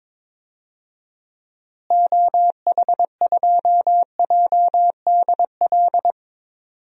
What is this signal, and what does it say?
Morse code, heard as a keyed tone, about 22 wpm, 700 Hz. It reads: OH2JDL
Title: DXCC-100 Call Signs - 6 Characters - Repeated 2x - 22wpm